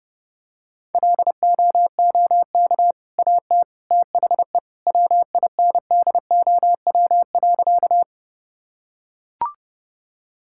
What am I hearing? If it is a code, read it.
Morse code, heard as a keyed tone, about 32 wpm, 700 Hz. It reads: LOOK AT THE WINDOW.
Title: Sentences from Top 1600 Words 30wpm